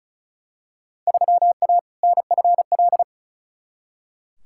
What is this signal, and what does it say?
Morse code, heard as a keyed tone, about 34 wpm, 700 Hz. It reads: 3A NFL
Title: Field Day Exchanges - 35wpm